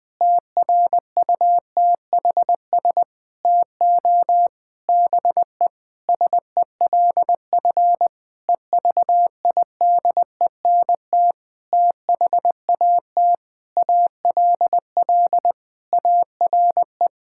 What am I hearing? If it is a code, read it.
Morse code, heard as a keyed tone, about 20 wpm, 700 Hz. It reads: TRUTHS TO BE SELF EVIDENT THAT ALL ARE
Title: Sentences with 9-Word Phrases - 20wpm